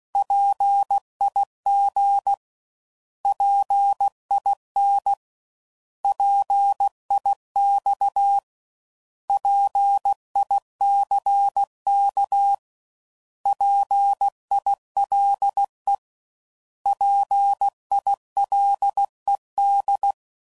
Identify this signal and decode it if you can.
Morse code, heard as a keyed tone, about 16 wpm, 800 Hz. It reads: PIG PIN PIX PICK PILE PILED
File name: Prefixes / By First Letter / Ps